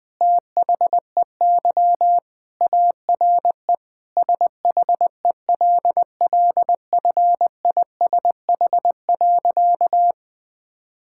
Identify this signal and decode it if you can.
Morse code, heard as a keyed tone, about 20 wpm, 700 Hz. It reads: THEY ARE SHELLFISH.
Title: All-Age Appropriate Jokes - 20wpm